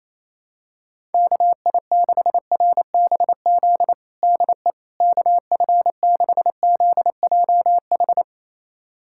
Morse code, heard as a keyed tone, about 28 wpm, 700 Hz. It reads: KI6RBZ DE KF6ZJH
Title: USA Callsign DE Callsign - 6 Characters - 28wpm